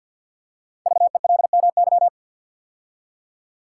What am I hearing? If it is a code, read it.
Morse code, heard as a keyed tone, about 50 wpm, 700 Hz. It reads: VELMY
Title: Code Groups - 5 Letters - 50wpm